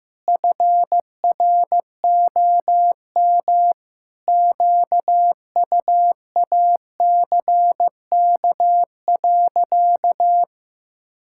Morse code, heard as a keyed tone, about 16 wpm, 700 Hz. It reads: FROM QUACK.
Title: Pangram Challenge v2 - 15wpm